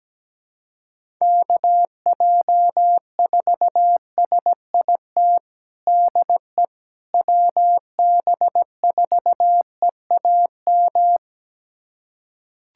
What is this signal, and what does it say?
Morse code, heard as a keyed tone, about 17 wpm, 700 Hz. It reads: KJ4SIT DE WB4EAM
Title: USA Callsign DE Callsign - 6 Characters - 17wpm